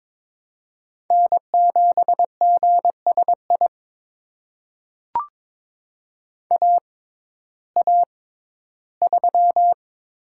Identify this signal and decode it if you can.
Morse code, heard as a keyed tone, about 22 wpm, spaced as for 13 wpm, 700 Hz. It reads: N7GSI AA3
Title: USA Call Signs - 5 Characters - 6x Character Spacing - 22wpm